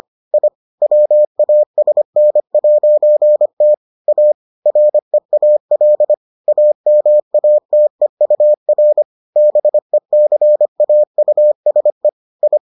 Morse code, heard as a keyed tone, about 25 wpm, 600 Hz. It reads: I WASN'T A REAL AMATEUR BECAUSE I